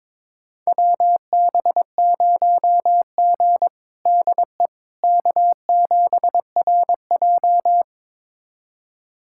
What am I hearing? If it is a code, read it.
Morse code, heard as a keyed tone, about 22 wpm, 700 Hz. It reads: WB0G DE K7RJ
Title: USA Callsign DE Callsign - 4 Characters - 22wpm